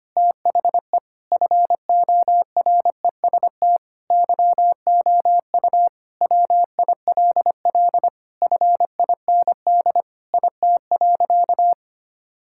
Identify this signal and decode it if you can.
Morse code, heard as a keyed tone, about 25 wpm, 700 Hz. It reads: THE FOREST YOU WILL FIND IT.
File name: practice / 25wpm / Sentences with 4-Word Phrases - 25wpm